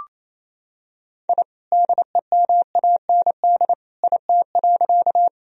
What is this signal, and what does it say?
Morse code, heard as a keyed tone, about 28 wpm, 700 Hz. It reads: I DEMAND IT.
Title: Sentences from Top 1000 Words - Review Part 4 - 28wpm